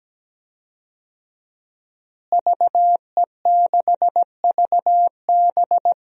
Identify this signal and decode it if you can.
Morse code, heard as a keyed tone, about 17 wpm, 700 Hz. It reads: VE6VB
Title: DXCC-100 Call Signs - 5 Characters - Repeated 3x - 17wpm